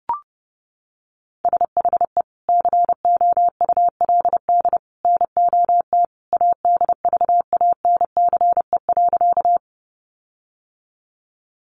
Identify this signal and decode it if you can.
Morse code, heard as a keyed tone, about 28 wpm, 700 Hz. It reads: SHE COULD NOT ADVANCE.
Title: Sentences from Top 1400 Words 30wpm